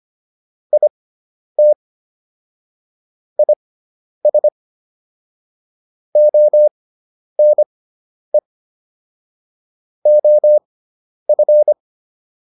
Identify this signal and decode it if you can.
Morse code, heard as a keyed tone, about 25 wpm, spaced as for 10 wpm, 600 Hz. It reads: IT IS ONE OF